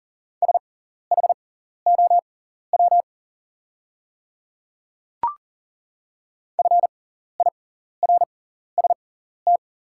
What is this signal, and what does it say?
Morse code, heard as a keyed tone, about 40 wpm, spaced as for 14 wpm, 700 Hz. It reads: SHOW FIRST
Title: 6x Character Spacing - Top 200 Words - 40wpm